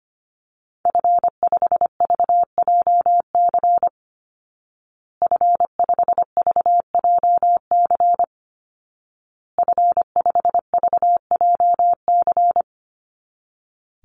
Morse code, heard as a keyed tone, about 25 wpm, 700 Hz. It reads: F5VJC F5VJC F5VJC
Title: Call Signs Repeated 2x - 25wpm